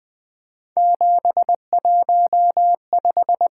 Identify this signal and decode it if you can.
Morse code, heard as a keyed tone, about 20 wpm, 700 Hz. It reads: 715
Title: Three Letter-Number Combinations - 20wpm